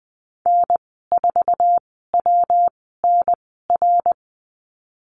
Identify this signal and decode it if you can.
Morse code, heard as a keyed tone, about 20 wpm, spaced as for 15 wpm, 700 Hz. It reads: N4WNR
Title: 200 Lesson 32g - Letter k - Call Signs 20wpm